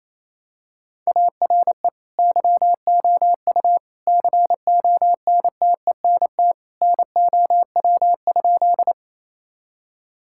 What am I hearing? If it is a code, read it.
Morse code, heard as a keyed tone, about 27 wpm, 700 Hz. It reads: ARE YOU CONTENT NOW?
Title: Sentences from Top 1000 Words - Review Part 6 - 28wpm